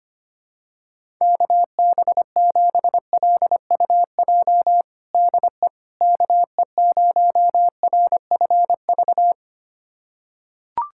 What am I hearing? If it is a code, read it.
Morse code, heard as a keyed tone, about 25 wpm, 700 Hz. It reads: KB7LUJ DE KE0RFV
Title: USA Callsign DE Callsign - 6 Characters - 25wpm